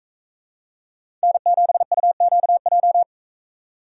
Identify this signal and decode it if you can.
Morse code, heard as a keyed tone, about 45 wpm, 700 Hz. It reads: N7UQJ